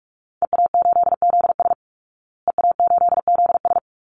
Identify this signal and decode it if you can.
Morse code, heard as a keyed tone, about 44 wpm, 700 Hz. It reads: EA8ZS EA8ZS
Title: Call Signs Repeated 2x - 45wpm